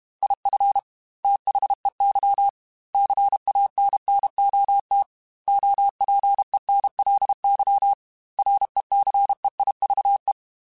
Morse code, heard as a keyed tone, about 32 wpm, 800 Hz. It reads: IF THEY CANNOT OPENLY RECEIVE